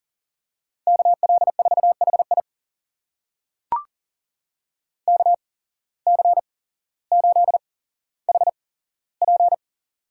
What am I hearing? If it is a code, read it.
Morse code, heard as a keyed tone, about 40 wpm, spaced as for 17 wpm, 700 Hz. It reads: KL4HI KC8HP
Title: USA Call Signs - 5 Characters - 8x Character Spacing - 40wpm